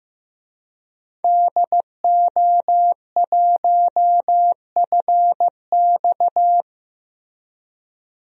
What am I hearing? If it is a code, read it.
Morse code, heard as a keyed tone, about 15 wpm, 700 Hz. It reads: DO1FX
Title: DXCC-100 Call Signs - 5 Characters - Repeated 3x - 15wpm